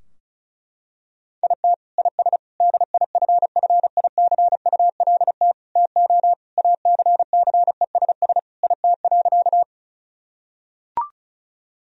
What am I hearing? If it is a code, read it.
Morse code, heard as a keyed tone, about 36 wpm, 700 Hz. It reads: IT IS DIFFICULT TO ACCESS IT.
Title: Sentences from Top 1000 Words - Review Part 4 - 35wpm